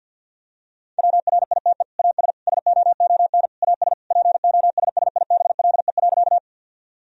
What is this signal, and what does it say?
Morse code, heard as a keyed tone, about 52 wpm, 700 Hz. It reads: WRITE AS SOON AS POSSIBLE.
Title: Sentences from Top 1000 Words - Review Part 1 - 50wpm